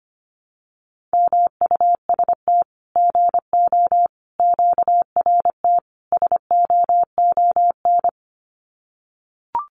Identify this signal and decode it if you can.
Morse code, heard as a keyed tone, about 25 wpm, 700 Hz. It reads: MUST GO QRT SOON
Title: Common QSO Phrases 25wpm